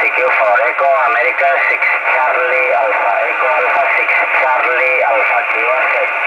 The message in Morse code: none